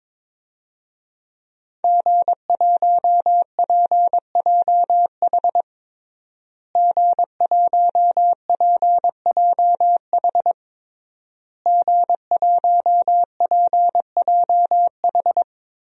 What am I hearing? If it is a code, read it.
Morse code, heard as a keyed tone, about 22 wpm, 700 Hz. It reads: G1PJH G1PJH G1PJH
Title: DXCC-100 Call Signs - 5 Characters - Repeated 3x - 22wpm